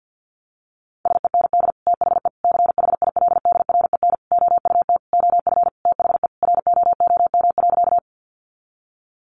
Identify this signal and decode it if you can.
Morse code, heard as a keyed tone, about 49 wpm, 700 Hz. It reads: SEND THE CHILDREN OUT OF THE ROOM.
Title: Sentences from Top 1500 Words 50wpm